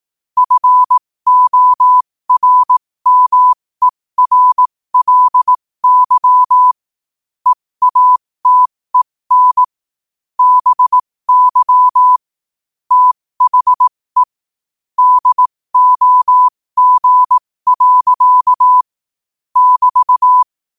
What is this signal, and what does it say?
Morse code, heard as a keyed tone, about 18 wpm, spaced as for 15 wpm, 1000 Hz. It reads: FORMERLY EATEN BY THE DOG. <BT>
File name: qotdfiles / 15wpm